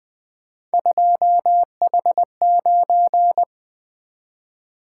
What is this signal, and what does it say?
Morse code, heard as a keyed tone, about 24 wpm, 700 Hz. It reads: 2H9
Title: Speed-Racing - Three Letter-Number Combinations - 20wpm